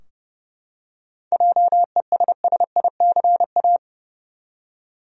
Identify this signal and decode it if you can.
Morse code, heard as a keyed tone, about 30 wpm, 700 Hz. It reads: JESSICA
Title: USA Common Names - 30wpm